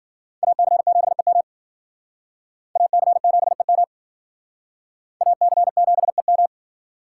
Morse code, heard as a keyed tone, about 51 wpm, 700 Hz. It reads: AC7EM AC7EM AC7EM